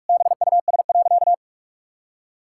Morse code, heard as a keyed tone, about 46 wpm, 700 Hz. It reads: BUS.
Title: Sentences from Top 1000 Words - Review Part 6 - 45wpm